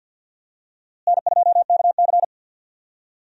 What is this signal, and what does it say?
Morse code, heard as a keyed tone, about 52 wpm, 700 Hz. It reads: N2KC
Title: USA Call Signs - 4 Characters - Repeated 3x - 50wpm